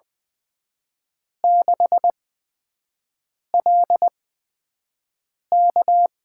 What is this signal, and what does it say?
Morse code, heard as a keyed tone, about 20 wpm, spaced as for 5 wpm, 700 Hz. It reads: 6LK
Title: USA Call Signs - 8x Character Spacing - 20wpm